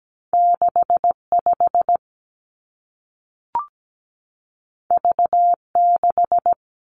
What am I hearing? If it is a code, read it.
Morse code, heard as a keyed tone, about 17 wpm, 700 Hz. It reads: 65 V6
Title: Two Challenging Letter-Number Combinations 17wpm